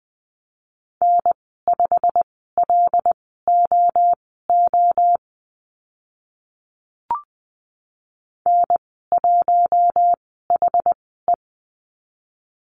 Morse code, heard as a keyed tone, about 20 wpm, spaced as for 15 wpm, 700 Hz. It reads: N5LOO N1HE
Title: 088 Lesson 15g - Number 5 - Call Signs 20wpm